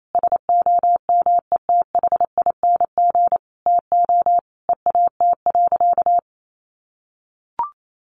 Morse code, heard as a keyed tone, about 28 wpm, 700 Hz. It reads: SOMETHING TO EAT.
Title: Sentences from Top 600 Words 28wpm